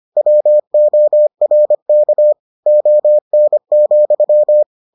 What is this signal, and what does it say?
Morse code, heard as a keyed tone, about 25 wpm, 600 Hz. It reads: WORK ON,